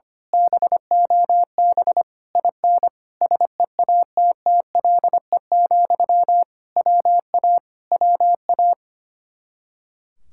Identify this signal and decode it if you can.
Morse code, heard as a keyed tone, about 24 wpm, 700 Hz. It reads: BOB IN SEATTLE, WA WA
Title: Speed-Racing Common QSO Phrases - 25wpm